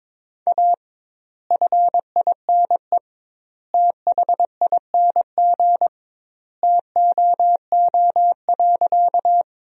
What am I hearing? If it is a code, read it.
Morse code, heard as a keyed tone, about 22 wpm, 700 Hz. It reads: A FINE THING TOO.
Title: 2x Word-Spacing - Sentences from Top 500 Words - 22wpm